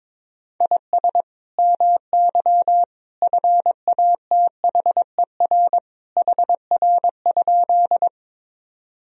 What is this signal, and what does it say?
Morse code, heard as a keyed tone, about 22 wpm, 700 Hz. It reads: IS MY FATHER HR?
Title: Sentences from Top 300 Words and Abbreviations 22wpm